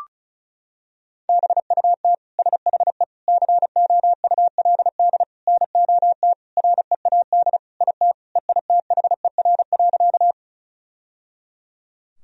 Morse code, heard as a keyed tone, about 34 wpm, 700 Hz. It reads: BUT SHE COULD NOT READ IT EITHER.
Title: Sentences from Top 1000 Words - Review Part 4 - 35wpm